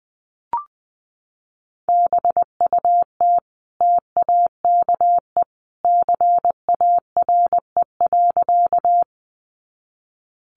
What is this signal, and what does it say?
Morse code, heard as a keyed tone, about 20 wpm, 700 Hz. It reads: BUT TAKE CARE.
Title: Sentences from Top 300 Words 20wpm